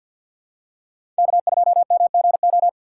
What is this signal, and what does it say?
Morse code, heard as a keyed tone, about 52 wpm, 700 Hz. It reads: K2MGO
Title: USA Call Signs - 5 Characters - 6x Character Spacing - 50wpm